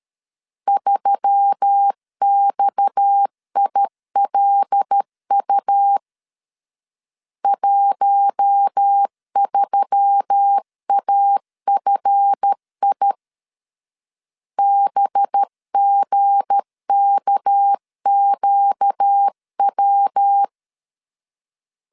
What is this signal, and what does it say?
Morse code, heard as a keyed tone, about 13 wpm, 800 Hz. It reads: 3XILU 13AFI BGKQW
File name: LEZIONI / 7 GF9Z / 7 GF 12wpm 2122